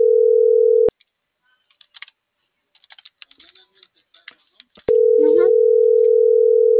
Ringback tone (North America), 440 + 480 Hz.